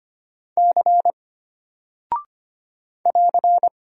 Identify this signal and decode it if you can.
Morse code, heard as a keyed tone, about 25 wpm, 700 Hz. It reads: C <AR>
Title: Single Character (Letter:Number:Punctuation:Prosign) Alt 25wpm